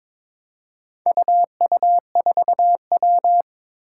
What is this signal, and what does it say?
Morse code, heard as a keyed tone, about 22 wpm, 700 Hz. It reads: UU4W
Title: Four Challenging Letter-Number Combinations 22wpm